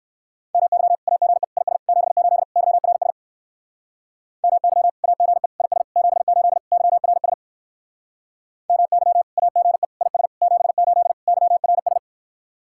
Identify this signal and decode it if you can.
Morse code, heard as a keyed tone, about 66 wpm, 700 Hz. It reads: MY AGE IS 78 YRS MY AGE IS 78 YRS MY AGE IS 78 YRS